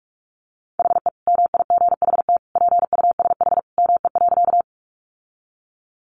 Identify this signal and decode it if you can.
Morse code, heard as a keyed tone, about 44 wpm, 700 Hz. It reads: HE MIGHT PUSH ME.
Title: Sentences from Top 1300 Words 45wpm